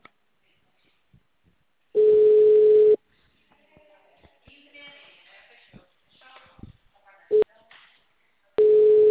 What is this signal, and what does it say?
Ringback tone (Europe, 425 Hz).